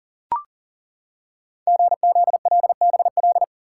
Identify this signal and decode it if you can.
Morse code, heard as a keyed tone, about 40 wpm, 700 Hz. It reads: GZLBL
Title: Code Groups - 5 Letters - 40wpm